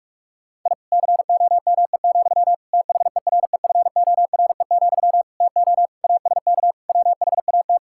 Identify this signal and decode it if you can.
Morse code, heard as a keyed tone, about 41 wpm, 700 Hz. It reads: I COME, THEREFORE, TO ASK WHAT
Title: Sentences from Top 1000 Words - Review Part 5 - 45wpm